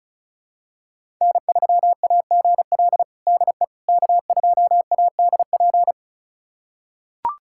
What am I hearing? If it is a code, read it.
Morse code, heard as a keyed tone, about 36 wpm, 700 Hz. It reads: N3AGL DE K2ADP
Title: USA Callsign DE Callsign - 5 Characters - 35wpm